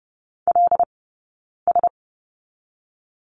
Morse code, heard as a keyed tone, about 30 wpm, 700 Hz. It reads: L S
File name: 092 Lesson 16d - Letter c - Two Characters 30wpm